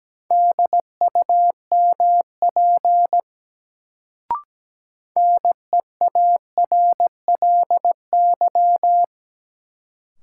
Morse code, heard as a keyed tone, about 17 wpm, 700 Hz. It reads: DUMP NEARLY
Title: Sentences from Top 1300 Words - New Word Drill - 17wpm